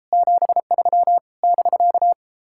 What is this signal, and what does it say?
Morse code, heard as a keyed tone, about 33 wpm, 700 Hz. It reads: 73 <BK>